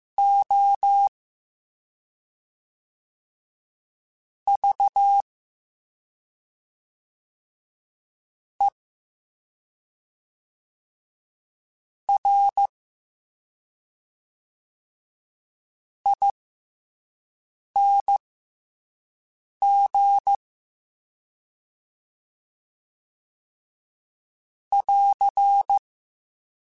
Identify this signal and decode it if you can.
Morse code, heard as a keyed tone, about 15 wpm, spaced as for 3 wpm, 800 Hz. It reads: OVERING <AR>